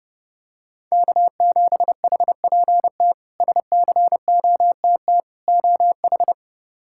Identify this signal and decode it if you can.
Morse code, heard as a keyed tone, about 32 wpm, 700 Hz. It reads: K7HPT SCOTT OH
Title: K1USN SST QSO Practice - 30wpm